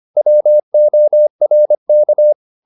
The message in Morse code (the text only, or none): WORK